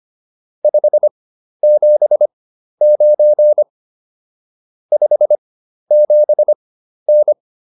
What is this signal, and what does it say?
Morse code, heard as a keyed tone, about 25 wpm, spaced as for 12 wpm, 600 Hz. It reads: 579 57N